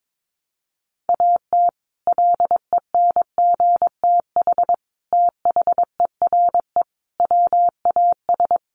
Morse code, heard as a keyed tone, about 22 wpm, 700 Hz. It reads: AT LENGTH THERE WAS
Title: Sentences from Top 1000 Words 22wpm